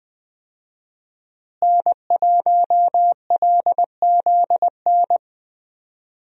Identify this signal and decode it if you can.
Morse code, heard as a keyed tone, about 20 wpm, 700 Hz. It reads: N1LZN
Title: USA Call Signs - 5 Characters - Repeated 3x - 20wpm